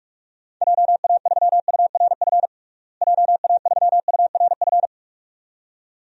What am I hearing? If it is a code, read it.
Morse code, heard as a keyed tone, about 46 wpm, 700 Hz. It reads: JA3URF JA3URF
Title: DXCC-100 Call Signs - 6 Characters - Repeated 2x - 45wpm